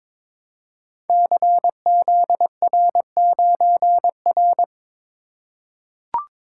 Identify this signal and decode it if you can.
Morse code, heard as a keyed tone, about 22 wpm, 700 Hz. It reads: CZR9R
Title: Code Groups - 5 Letter:Numbers - 22wpm